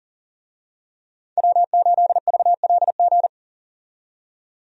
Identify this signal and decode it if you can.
Morse code, heard as a keyed tone, about 40 wpm, 700 Hz. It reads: W8VLG
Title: USA Call Signs - 5 Characters - 40wpm